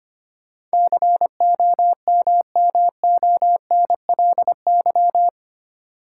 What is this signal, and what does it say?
Morse code, heard as a keyed tone, about 25 wpm, 700 Hz. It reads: COMMONLY
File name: 8 Letter Words - First 3 Letters Given - 25wpm